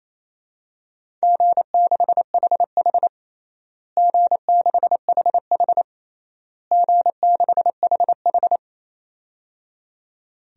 Morse code, heard as a keyed tone, about 27 wpm, 700 Hz. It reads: G6HH G6HH G6HH
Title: DXCC-100 Call Signs - 4 Characters - Repeated 3x - 28wpm